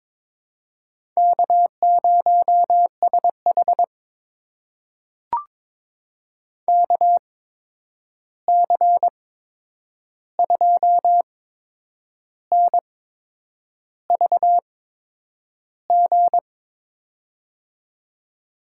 Morse code, heard as a keyed tone, about 22 wpm, spaced as for 8 wpm, 700 Hz. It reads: K0SH KC2NVG
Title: USA Call Signs - 8x Character Spacing - 22wpm